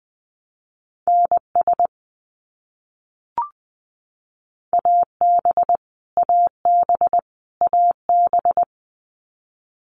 Morse code, heard as a keyed tone, about 20 wpm, 700 Hz. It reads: NS AB AB AB
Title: Canadian Provinces and Territories - Repeated 3x - 20wpm